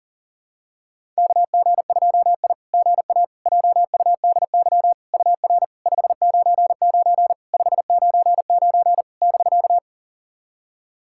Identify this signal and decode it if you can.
Morse code, heard as a keyed tone, about 40 wpm, 660 Hz. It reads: KG2I GA JUDY UR 599 599 <BK>
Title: SOTA QSO Practice 40wpm